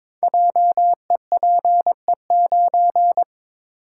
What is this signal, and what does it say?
Morse code, heard as a keyed tone, about 21 wpm, 700 Hz. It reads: JEPE9
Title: Code Groups - 5 Letter:Numbers - 22wpm